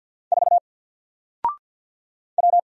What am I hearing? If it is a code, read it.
Morse code, heard as a keyed tone, about 50 wpm, 700 Hz. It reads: 4 W
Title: Single Letter-Number 50wpm